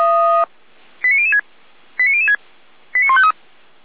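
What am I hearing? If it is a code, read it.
Five-tone selective call (ZVEI), heard as quick runs of five tones, tones of about 70 ms. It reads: 89006 89006 89153
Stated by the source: ZVEI five-tone selective call received on SDR by the Feuerwehr Orlamünde fire brigade.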